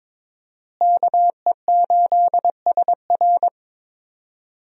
Morse code, heard as a keyed tone, about 22 wpm, 700 Hz. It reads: KE8SR
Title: USA Call Signs - 5 Characters - Repeated 3x - 22wpm